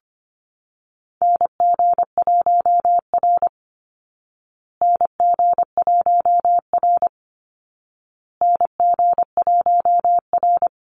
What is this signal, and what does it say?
Morse code, heard as a keyed tone, about 25 wpm, 700 Hz. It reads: NG1R NG1R NG1R
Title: Call Signs Repeated 2x - 25wpm